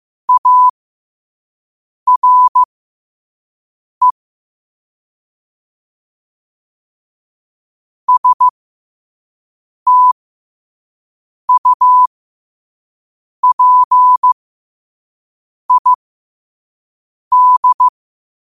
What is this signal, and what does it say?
Morse code, heard as a keyed tone, about 15 wpm, spaced as for 5 wpm, 1000 Hz. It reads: ARE STUPID
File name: qotdfiles / 5wpm